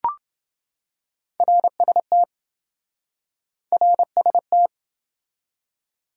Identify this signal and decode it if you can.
Morse code, heard as a keyed tone, about 26 wpm, 700 Hz. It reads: RST RST